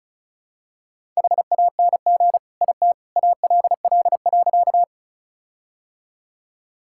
Morse code, heard as a keyed tone, about 36 wpm, 700 Hz. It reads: HANG IT ALL.
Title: Sentences from Top 1600 Words 35wpm